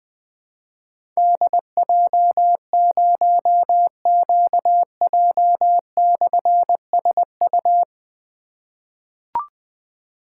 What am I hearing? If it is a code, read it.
Morse code, heard as a keyed tone, about 21 wpm, 700 Hz. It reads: DJ0QJ/SU
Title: Difficult Call Signs 20wpm